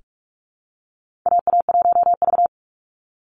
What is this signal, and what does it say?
Morse code, heard as a keyed tone, about 44 wpm, 700 Hz. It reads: AA1V
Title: Call Signs Repeated 3x - 45wpm